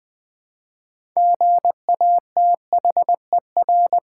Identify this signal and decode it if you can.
Morse code, heard as a keyed tone, about 20 wpm, 700 Hz. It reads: GATHER
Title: 6 Letter Words - First 3 Letters Given - 20wpm